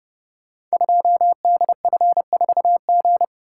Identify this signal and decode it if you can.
Morse code, heard as a keyed tone, about 30 wpm, 700 Hz. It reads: 2DF4G
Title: Code Groups - 5 Letter:Numbers - 30wpm